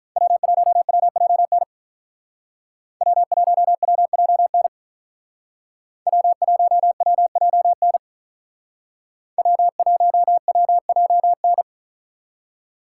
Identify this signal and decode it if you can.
Morse code, heard as a keyed tone, about 44 wpm, 700 Hz. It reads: W1WJN W1WJN W1WJN W1WJN